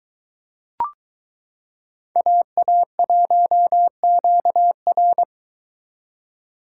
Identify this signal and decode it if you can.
Morse code, heard as a keyed tone, about 22 wpm, 700 Hz. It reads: AA1QR